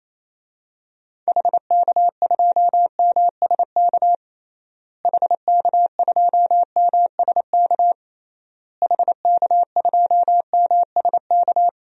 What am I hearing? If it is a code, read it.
Morse code, heard as a keyed tone, about 28 wpm, 700 Hz. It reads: HK2MSK HK2MSK HK2MSK
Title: DXCC-100 Call Signs - 6 Characters - Repeated 3x - 28wpm